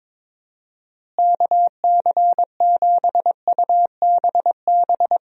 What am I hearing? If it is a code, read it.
Morse code, heard as a keyed tone, about 21 wpm, 700 Hz. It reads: KC7UBB
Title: USA Call Signs - 6 Characters - Repeated 2x - 22wpm